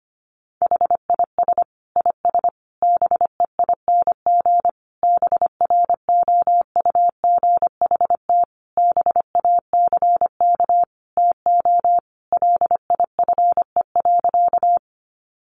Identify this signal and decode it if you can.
Morse code, heard as a keyed tone, about 25 wpm, 700 Hz. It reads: HIS IS BEING BROUGHT BACK TO LIFE.
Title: Sentences from Top 600 Words 25wpm